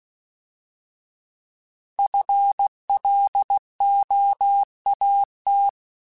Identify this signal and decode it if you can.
Morse code, heard as a keyed tone, about 16 wpm, 800 Hz. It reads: FLOAT
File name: Alphabetical / Alphabetical mp3s / Fs @15